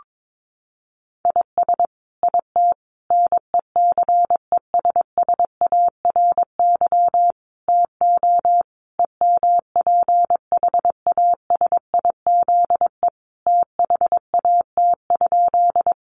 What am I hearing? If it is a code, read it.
Morse code, heard as a keyed tone, about 22 wpm, 700 Hz. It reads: IS IT NECESSARY TO EMPHASIZE THAT?
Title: Sentences from Top 1500 Words 22wpm